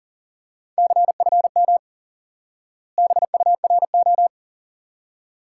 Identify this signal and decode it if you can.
Morse code, heard as a keyed tone, about 40 wpm, 700 Hz. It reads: CFM BURO
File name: IQE Course 3 - Lesson 89 - YAGI - Two Elements - 40wpm